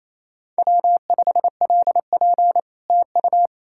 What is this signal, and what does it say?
Morse code, heard as a keyed tone, about 38 wpm, 700 Hz. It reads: W5LP TU